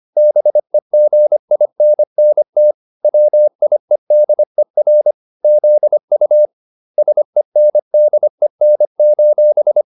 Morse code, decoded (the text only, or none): BEGINNT WIEDER ZU SENDEN: